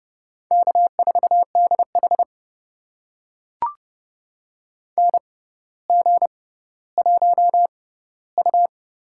Morse code, heard as a keyed tone, about 30 wpm, spaced as for 15 wpm, 700 Hz. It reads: K4DH NG1U
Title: USA Call Signs - 4 Characters - 6x Character Spacing - 30wpm